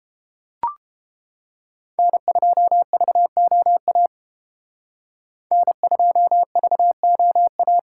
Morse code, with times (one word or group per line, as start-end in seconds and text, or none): 2.0-4.1 N2VOA
5.5-7.8 N2VOA